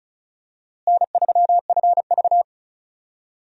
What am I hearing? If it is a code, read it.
Morse code, heard as a keyed tone, about 36 wpm, 700 Hz. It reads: N3FV
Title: USA Call Signs - 4 Characters - Repeated 3x - 35wpm